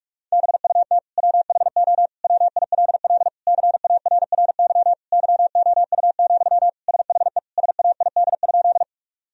Morse code, heard as a keyed tone, about 41 wpm, 700 Hz. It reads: BUT WHO WILL CARRY YOU, SHE SAID?
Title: Sentences from Top 1000 Words - Review Part 1 - 45wpm